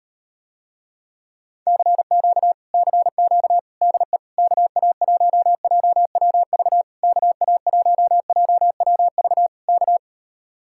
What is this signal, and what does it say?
Morse code, heard as a keyed tone, about 38 wpm, 700 Hz. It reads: CQ CQ DE KA1JWV KA1JWV K